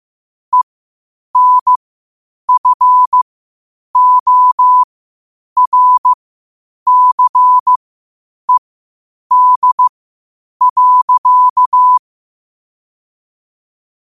Morse code, heard as a keyed tone, about 15 wpm, spaced as for 8 wpm, 1000 Hz. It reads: ENFORCED.
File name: qotdfiles / 8wpm